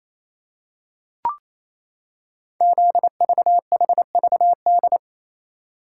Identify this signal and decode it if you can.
Morse code, heard as a keyed tone, about 28 wpm, 700 Hz. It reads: ZVHVD
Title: Code Groups - 5 Letters - 28wpm